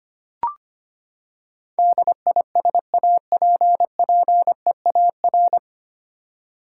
Morse code, heard as a keyed tone, about 25 wpm, 700 Hz. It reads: DISAPPEAR
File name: Nine Letter Words - 25wpm